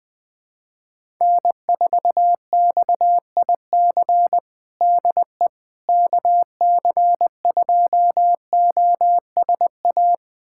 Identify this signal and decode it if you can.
Morse code, heard as a keyed tone, about 20 wpm, 700 Hz. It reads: N4XIC DE KC2OSA
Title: USA Callsign DE Callsign 20wpm